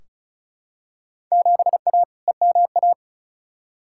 Morse code, heard as a keyed tone, about 36 wpm, 700 Hz. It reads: 7A EMA
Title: Field Day Exchanges Encore - 35wpm